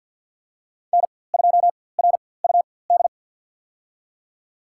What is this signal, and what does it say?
Morse code, heard as a keyed tone, about 53 wpm, spaced as for 24 wpm, 700 Hz. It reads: N2RUD
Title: USA Call Signs - 4x Character Spacing - 50wpm